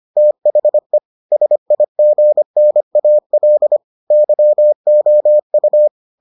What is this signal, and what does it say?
Morse code, heard as a keyed tone, about 25 wpm, 600 Hz. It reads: THE SIGNAL YOU